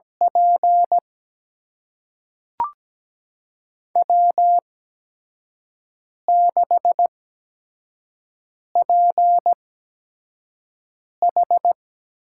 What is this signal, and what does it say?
Morse code, heard as a keyed tone, about 17 wpm, spaced as for 5 wpm, 700 Hz. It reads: P W6PH
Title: USA Call Signs - 8x Character Spacing - 17wpm